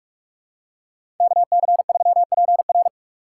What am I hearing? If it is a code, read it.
Morse code, heard as a keyed tone, about 46 wpm, 700 Hz. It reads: KC3PR
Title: USA Call Signs - 5 Characters - 6x Character Spacing - 45wpm